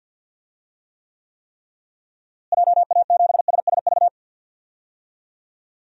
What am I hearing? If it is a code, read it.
Morse code, heard as a keyed tone, about 52 wpm, 700 Hz. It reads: JA7SSV
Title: DXCC-100 Call Signs - 6 Characters - Repeated 3x - 50wpm